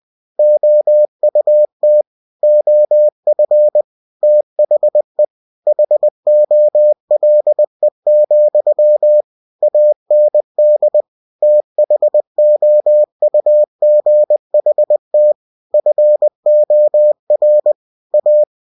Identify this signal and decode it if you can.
Morse code, heard as a keyed tone, about 20 wpm, 600 Hz. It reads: OUT OF THE HOLE, AND THOUGHT FOR A
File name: cw-books / winnie-the-pooh / winnie-20wpm-page-0014